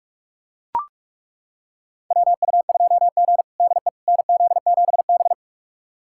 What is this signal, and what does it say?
Morse code, heard as a keyed tone, about 42 wpm, 700 Hz. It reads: WU2G DE NZ7B
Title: USA Callsign DE Callsign - 4 Characters - 45wpm